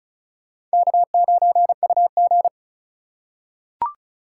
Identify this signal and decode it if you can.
Morse code, heard as a keyed tone, about 36 wpm, 700 Hz. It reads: K9UG
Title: USA Call Signs - 2x Character Spacing - 35wpm